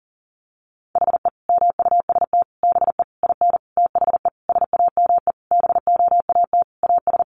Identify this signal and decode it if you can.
Morse code, heard as a keyed tone, about 40 wpm, 700 Hz. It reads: HE MUST BE IN THE SAME BOAT AS
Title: Sentences from Top 500 Words 40wpm